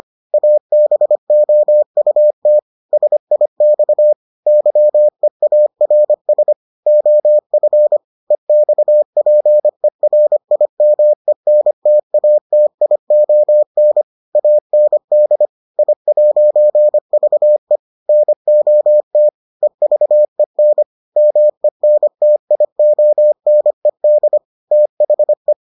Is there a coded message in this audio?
Yes: Morse code from 0.3 to 25.5 s, heard as a keyed tone, about 25 wpm, 600 Hz. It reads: ABOUT SIX YEARS OF EXPERIMENTATION AND I'VE NOT EVEN MENTIONED THE